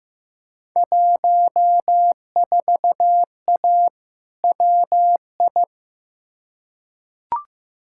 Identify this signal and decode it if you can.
Morse code, heard as a keyed tone, about 16 wpm, 700 Hz. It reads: 14A WI
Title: Field Day Exchanges Encore - 15wpm